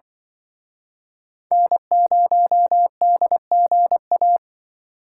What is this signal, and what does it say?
Morse code, heard as a keyed tone, about 24 wpm, 700 Hz. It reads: N0DGA
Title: Speed Racing - USA Call Signs 20wpm